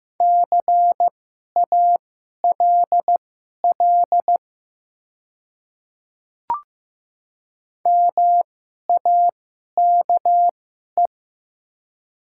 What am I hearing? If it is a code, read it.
Morse code, heard as a keyed tone, about 15 wpm, spaced as for 11 wpm, 700 Hz. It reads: CALL MAKE
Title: 2x Character Spacing - Top 200 Words - 15wpm